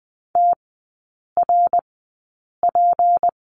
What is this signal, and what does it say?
Morse code, heard as a keyed tone, about 20 wpm, spaced as for 8 wpm, 690 Hz. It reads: TRP
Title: 212 Lesson 34e - Number 8 - Three Characters 20wpm